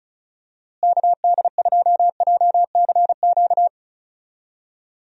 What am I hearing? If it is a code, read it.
Morse code, heard as a keyed tone, about 36 wpm, 700 Hz. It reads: KD2JCQ
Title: USA Call Signs - 6 Characters - Repeated 3x - 35wpm